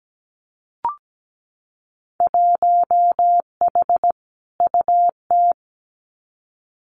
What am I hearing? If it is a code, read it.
Morse code, heard as a keyed tone, about 17 wpm, 700 Hz. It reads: 1H UT
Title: Winter Field Day Exchanges 17wpm